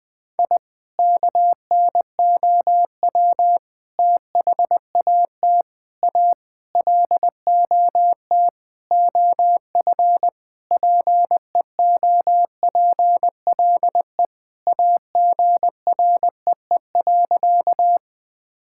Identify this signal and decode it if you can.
Morse code, heard as a keyed tone, about 20 wpm, 690 Hz. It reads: I KNOW THAT A LOT OF PEOPLE AGREE.
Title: Sentences with 4-Word Phrases - Spoken Beginning - 20wpm